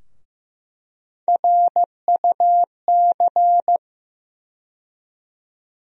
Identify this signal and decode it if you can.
Morse code, heard as a keyed tone, about 15 wpm, 700 Hz. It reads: RUC
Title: USA Call Sign Postfixes - 3 Letters 15wpm